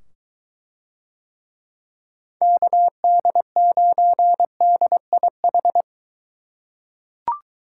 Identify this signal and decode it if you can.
Morse code, heard as a keyed tone, about 22 wpm, 700 Hz. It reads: KD9DIH